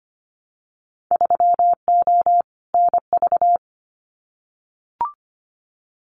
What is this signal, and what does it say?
Morse code, heard as a keyed tone, about 25 wpm, 700 Hz. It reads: 3O NV
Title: Winter Field Day Exchanges Encore 25wpm